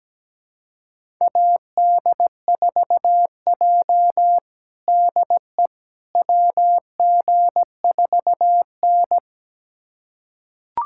Morse code, heard as a keyed tone, about 17 wpm, 700 Hz. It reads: AD4J DE WG4N
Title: USA Callsign DE Callsign - 4 Characters - 17wpm